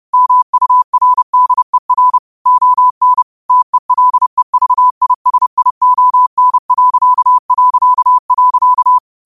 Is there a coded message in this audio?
Yes: Morse code from 0.1 to 9.0 s, heard as a keyed tone, about 30 wpm, 1000 Hz. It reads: MURDER ON TELEVISION...